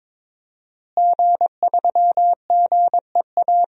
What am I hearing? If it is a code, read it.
Morse code, heard as a keyed tone, about 22 wpm, 700 Hz. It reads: G3GEA
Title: DXCC-100 Call Signs - 5 Characters - Repeated 3x - 22wpm